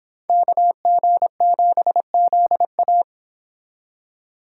Morse code, heard as a keyed tone, about 25 wpm, 700 Hz. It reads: KG7ZA